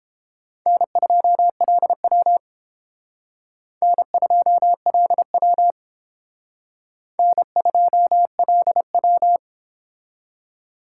Morse code, heard as a keyed tone, about 29 wpm, 700 Hz. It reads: N2LW N2LW N2LW